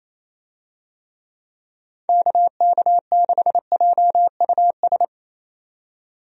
Morse code, heard as a keyed tone, about 28 wpm, 700 Hz. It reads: KK6JUS
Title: USA Call Signs - 6 Characters - 28wpm